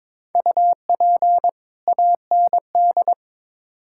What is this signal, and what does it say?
Morse code, heard as a keyed tone, about 22 wpm, 700 Hz. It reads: UP AND
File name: Build A Phrase - 2-Word Phrases - 22wpm